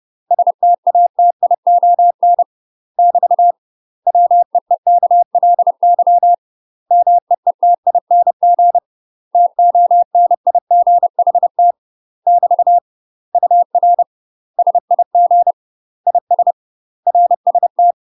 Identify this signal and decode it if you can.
Morse code, heard as a keyed tone, about 30 wpm, 700 Hz. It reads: STATION <BT> WEEKLY MEETING TONIGHT <BT> UR SIG IS RST